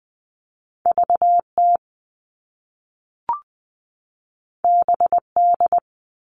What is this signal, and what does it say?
Morse code, heard as a keyed tone, about 20 wpm, 700 Hz. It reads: VT BD
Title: All Cut Numbers - 2-Digit 20wpm